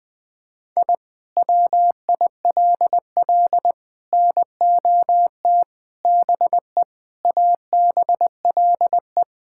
Morse code, heard as a keyed tone, about 20 wpm, 700 Hz. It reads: I WILL NOT BE ABLE
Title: Sentences with 5-Word Phrases - 20wpm